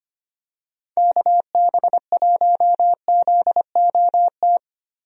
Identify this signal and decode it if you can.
Morse code, heard as a keyed tone, about 25 wpm, 700 Hz. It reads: KB1ZOT
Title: USA Call Signs - 6 Characters - Repeated 3x - 25wpm